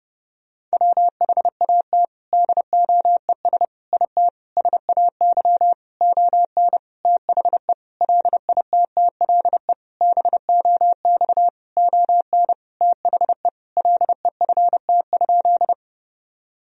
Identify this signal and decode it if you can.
Morse code, heard as a keyed tone, about 30 wpm, 700 Hz. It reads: WHAT DOES IT SAY ON THE LITTLE BOX ON THE LEFT?
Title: Sentences from Top 1000 Words - Review Part 2 - 30wpm